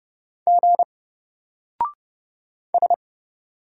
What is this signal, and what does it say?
Morse code, heard as a keyed tone, about 30 wpm, 700 Hz. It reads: G S
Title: Single Character (Letter:Number:Punctuation:Prosign) 30wpm